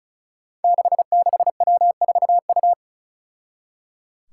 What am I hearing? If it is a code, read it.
Morse code, heard as a keyed tone, about 36 wpm, 700 Hz. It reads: 66W4U
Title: Five Challenging Letter-Number Combinations 35wpm